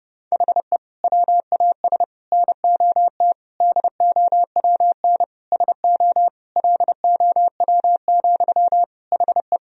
Morse code, heard as a keyed tone, about 30 wpm, 700 Hz. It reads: HE WAS NOT DOWN SO LOW, HE
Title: Sentences from Top 1000 Words - Review Part 1 - 30wpm